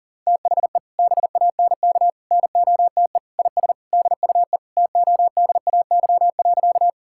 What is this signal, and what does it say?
Morse code, heard as a keyed tone, about 40 wpm, 700 Hz. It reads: THE BANK NOTE IS DUE TODAY.
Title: Sentences from Top 1000 Words - Review Part 6 - 40wpm